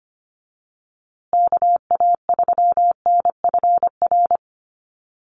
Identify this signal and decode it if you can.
Morse code, heard as a keyed tone, about 25 wpm, 700 Hz. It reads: KA3NFR
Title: USA Call Signs Repeated 3x 25wpm